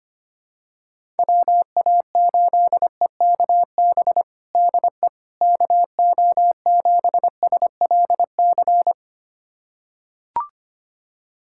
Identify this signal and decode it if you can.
Morse code, heard as a keyed tone, about 25 wpm, 700 Hz. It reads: WA8EKB DE KO7SLC
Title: USA Callsign DE Callsign - 6 Characters - 25wpm